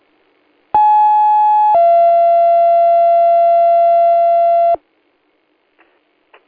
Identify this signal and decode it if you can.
Two-tone page, heard as a one-second tone, then a three-second tone, about 850 then 680 Hz. It is heard from 0.7 to 4.8 s.